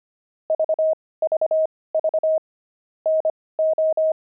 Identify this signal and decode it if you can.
Morse code, heard as a keyed tone, about 25 wpm, spaced as for 18 wpm, 630 Hz. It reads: VVV NO
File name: pangram-fs-18-cs-25-kidding